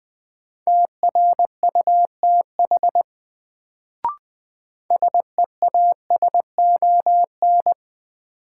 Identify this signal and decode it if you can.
Morse code, heard as a keyed tone, about 20 wpm, 700 Hz. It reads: TRUTH SEASON
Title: Sets of 5 Words - New Word Drill - 20wpm